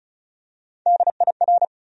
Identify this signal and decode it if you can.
Morse code, heard as a keyed tone, about 36 wpm, 700 Hz. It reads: DIR
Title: USA Call Sign Postfixes - 3 Letters 35wpm